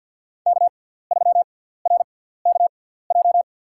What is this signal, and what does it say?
Morse code, heard as a keyed tone, about 53 wpm, spaced as for 17 wpm, 690 Hz. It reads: K3RKJ
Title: USA Call Signs - 5 Characters - 6x Character Spacing - 50wpm